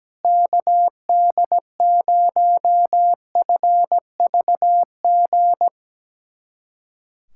Morse code, heard as a keyed tone, about 17 wpm, 700 Hz. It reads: KD0FVG
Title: USA Callsign DE Callsign - 6 Characters - 17wpm